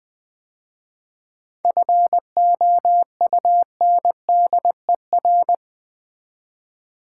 Morse code, heard as a keyed tone, about 20 wpm, 700 Hz. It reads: FOUNDER
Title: 7 Letter Words - First 3 Letters Given - 20wpm